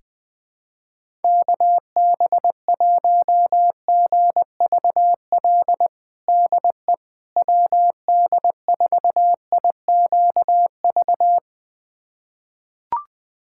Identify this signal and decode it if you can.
Morse code, heard as a keyed tone, about 20 wpm, 700 Hz. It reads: KB1GVL DE WD4IQV
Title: USA Callsign DE Callsign - 6 Characters - 20wpm